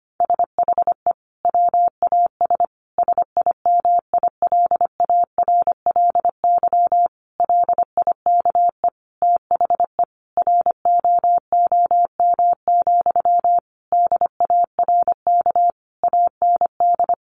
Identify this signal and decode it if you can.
Morse code, heard as a keyed tone, about 25 wpm, 700 Hz. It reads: SHE WAS SIMILARLY LIKE THE ROOM, DARK AND
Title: Sentences from Top 1400 Words 25wpm